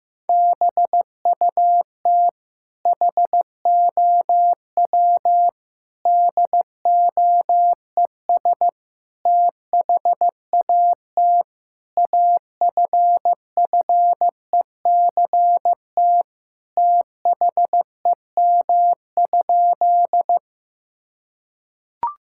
Morse code, heard as a keyed tone, about 15 wpm, 700 Hz. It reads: BUT HOW DOES THAT AFFECT THEM?
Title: Sentences from Top 1000 Words - Review Part 6 - 15wpm